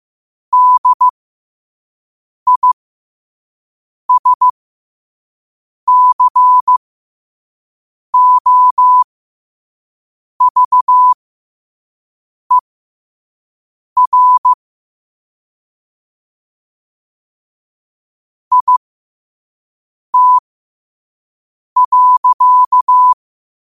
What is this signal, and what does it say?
Morse code, heard as a keyed tone, about 15 wpm, spaced as for 5 wpm, 1000 Hz. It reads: DISCOVER IT.
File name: qotdfiles / 5wpm